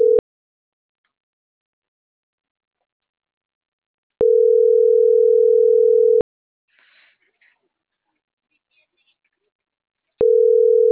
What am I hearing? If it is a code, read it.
Ringback tone (North America), 440 + 480 Hz.